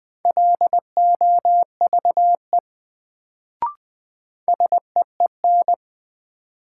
Morse code, heard as a keyed tone, about 20 wpm, 700 Hz. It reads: LOVE SEEN
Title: Sets of 5 Words Encore - New Word Drill - 20wpm